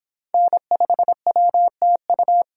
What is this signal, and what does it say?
Morse code, heard as a keyed tone, about 28 wpm, 700 Hz. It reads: N5WTU